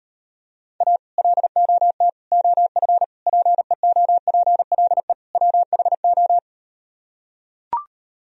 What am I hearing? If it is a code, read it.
Morse code, heard as a keyed tone, about 38 wpm, 700 Hz. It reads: A LOT OF PEOPLE WHO